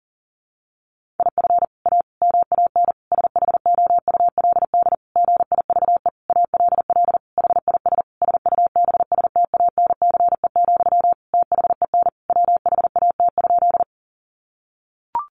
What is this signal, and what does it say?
Morse code, heard as a keyed tone, about 37 wpm, 700 Hz. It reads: IF A MAN SHOULD GIVE ALL HIS SUBSTANCE, THEN WHAT?
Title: Sentences from Top 1500 Words 40wpm